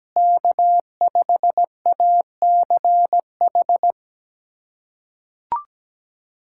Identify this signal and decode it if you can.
Morse code, heard as a keyed tone, about 17 wpm, 700 Hz. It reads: K5ACH
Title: USA Callsign DE Callsign - 5 Characters - 17wpm